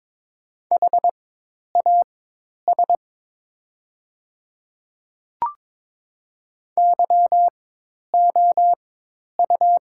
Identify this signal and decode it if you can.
Morse code, heard as a keyed tone, about 22 wpm, spaced as for 10 wpm, 700 Hz. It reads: HAS YOU
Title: 4x Character Spacing - Top 100 Words - 22wpm